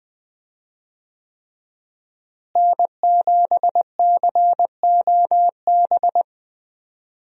Morse code, heard as a keyed tone, about 20 wpm, 700 Hz. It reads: N7COB
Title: USA Call Signs - 5 Characters - Repeated 3x - 20wpm